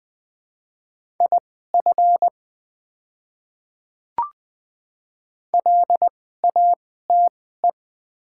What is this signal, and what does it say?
Morse code, heard as a keyed tone, about 20 wpm, spaced as for 14 wpm, 700 Hz. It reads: IF LATE
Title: 2x Character Spacing - Top 300 Words - 20wpm